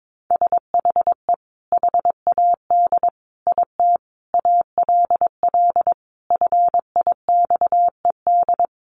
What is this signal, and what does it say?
Morse code, heard as a keyed tone, about 22 wpm, 700 Hz. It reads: SHE HAD IT ALL FIXED
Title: Sentences from Top 1100 Words 22wpm